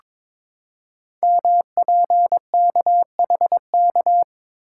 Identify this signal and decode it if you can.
Morse code, heard as a keyed tone, about 22 wpm, 700 Hz. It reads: MPKHK
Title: Code Groups - 5 Letter:Numbers - 22wpm